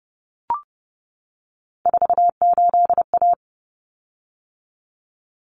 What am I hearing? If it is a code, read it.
Morse code, heard as a keyed tone, about 30 wpm, 700 Hz. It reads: 48A
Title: Common Cut Numbers - 3-Digit 30wpm